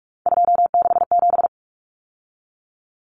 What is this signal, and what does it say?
Morse code, heard as a keyed tone, about 44 wpm, 700 Hz. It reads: 267
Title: Numbers - 3-Digit 45wpm